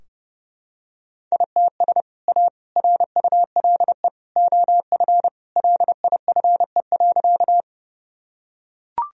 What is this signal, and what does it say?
Morse code, heard as a keyed tone, about 30 wpm, 700 Hz. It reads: ITS A RULE OF LIFE.
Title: Sentences from Top 1000 Words - Review Part 2 - 30wpm